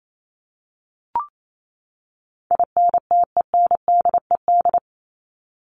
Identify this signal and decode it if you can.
Morse code, heard as a keyed tone, about 28 wpm, 700 Hz. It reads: INTENDED
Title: ED Suffix - Top 100 Words with Base Words 28wpm